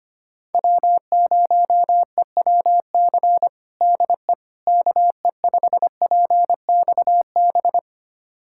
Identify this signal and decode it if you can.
Morse code, heard as a keyed tone, about 25 wpm, 700 Hz. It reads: W0EWC DE KE5PXB
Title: USA Callsign DE Callsign 25wpm